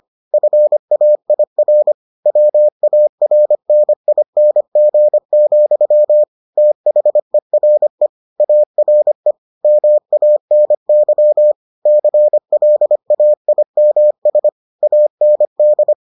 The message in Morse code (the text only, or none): FAIR WARNING, THERE ARE MANY CLAIMS AND